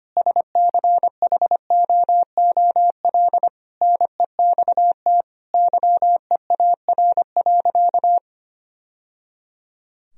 Morse code, heard as a keyed tone, about 25 wpm, 700 Hz. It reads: SCHOOL NEXT YEAR.
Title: Sentences from Top 1000 Words - Review Part 3 - 25wpm